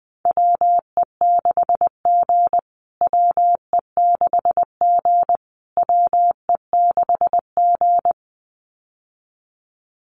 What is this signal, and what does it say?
Morse code, heard as a keyed tone, about 19 wpm, 700 Hz. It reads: WE6G WE6G WE6G
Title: USA Call Signs Repeated 3x 20wpm